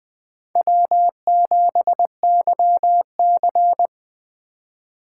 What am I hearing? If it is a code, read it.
Morse code, heard as a keyed tone, about 20 wpm, 700 Hz. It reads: W7YC
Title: USA Call Signs - 4 Characters - 4x Character Spacing - 20wpm